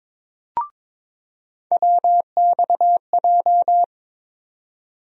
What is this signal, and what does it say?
Morse code, heard as a keyed tone, about 21 wpm, 700 Hz. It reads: WXJ
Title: USA Call Sign Postfixes - 3 Letters 22wpm